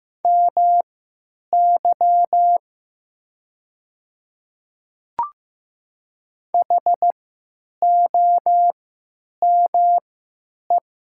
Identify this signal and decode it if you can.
Morse code, heard as a keyed tone, about 15 wpm, spaced as for 9 wpm, 690 Hz. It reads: MY HOME
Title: 3x Character Spacing - Top 300 Words - 15wpm